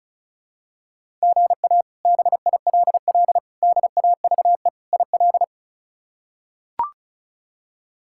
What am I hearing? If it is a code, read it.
Morse code, heard as a keyed tone, about 36 wpm, 700 Hz. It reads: GA BILL DAVE IL
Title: K1USN SST QSO Practice - 35wpm